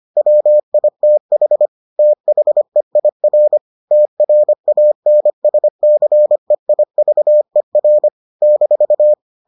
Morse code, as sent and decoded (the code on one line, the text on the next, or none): .-- .. - .... / - .... .. .-. / - .-. .- -. ... -.-. .. ...- . .-. / -....-
WITH THEIR TRANSCEIVER -